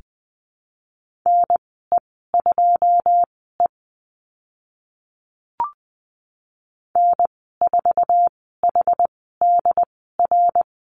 Morse code, heard as a keyed tone, about 20 wpm, spaced as for 14 wpm, 700 Hz. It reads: NE2E N4HDR
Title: 088 Lesson 15g - Number 5 - Call Signs 20wpm